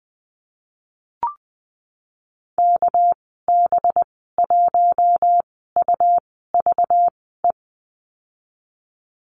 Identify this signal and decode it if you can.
Morse code, heard as a keyed tone, about 20 wpm, spaced as for 14 wpm, 700 Hz. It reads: KB1UVE
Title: 200 Lesson 32g - Letter k - Call Signs 20wpm